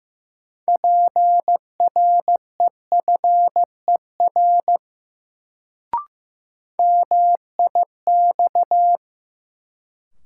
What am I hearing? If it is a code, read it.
Morse code, heard as a keyed tone, about 16 wpm, 700 Hz. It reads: PREFER MIX
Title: Sentences from Top 1100 Words - New Word Drill Alt - 15wpm